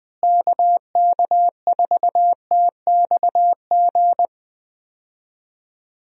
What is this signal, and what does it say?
Morse code, heard as a keyed tone, about 20 wpm, 700 Hz. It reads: KK4TXG
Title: USA Call Signs - 6 Characters - Repeated 2x - 20wpm